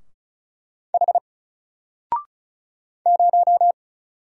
Morse code, heard as a keyed tone, about 37 wpm, 700 Hz. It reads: H 0
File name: Single-Letter-Number-35wpm